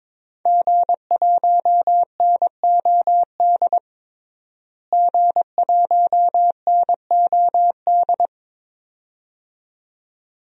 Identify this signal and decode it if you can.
Morse code, heard as a keyed tone, about 21 wpm, 700 Hz. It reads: G1NOD G1NOD
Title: DXCC-100 Call Signs - 5 Characters - Repeated 2x - 22wpm